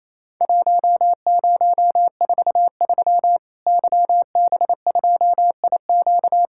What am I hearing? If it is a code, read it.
Morse code, heard as a keyed tone, about 28 wpm, 700 Hz. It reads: 1043 YB2IQ
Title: Worked All Europe DX Contest - QTC Exchange - 28wpm